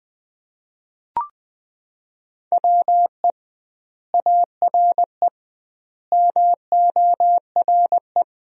Morse code, heard as a keyed tone, about 20 wpm, 700 Hz. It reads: WE ARE MORE
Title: 2x Word-Spacing - Sentences from Top 500 Words - 20wpm